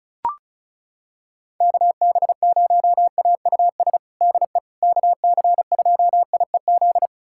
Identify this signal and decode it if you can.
Morse code, heard as a keyed tone, about 34 wpm, 700 Hz. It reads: KB0AUS DE KC2IEZ
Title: USA Callsign DE Callsign 35wpm